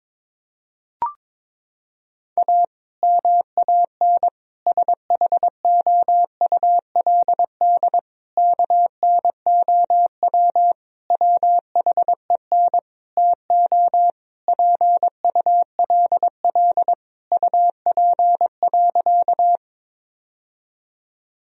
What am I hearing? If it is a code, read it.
Morse code, heard as a keyed tone, about 22 wpm, 700 Hz. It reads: A MAN SHOULD KNOW WHEN TO PULL UP.
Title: Sentences from Top 1000 Words - Review Part 2 - 22wpm